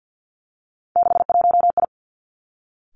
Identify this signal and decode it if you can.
Morse code, heard as a keyed tone, about 50 wpm, 700 Hz. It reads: 61I
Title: Three Challenging Letter-Number Combinations 50wpm